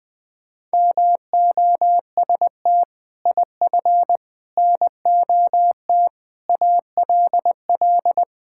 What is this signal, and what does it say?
Morse code, heard as a keyed tone, about 20 wpm, 700 Hz. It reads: MOST IF NOT ALL
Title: Sets of 6 Words - 20wpm